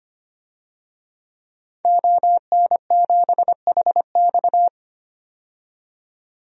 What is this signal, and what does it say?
Morse code, heard as a keyed tone, about 25 wpm, 700 Hz. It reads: ON7HX
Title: DXCC-100 Call Signs - 5 Characters - 25wpm